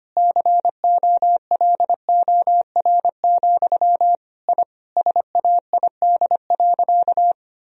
Morse code, heard as a keyed tone, about 25 wpm, 700 Hz. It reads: COLOR, I SAID.
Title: Sentences from Top 1000 Words - Review Part 2 - 25wpm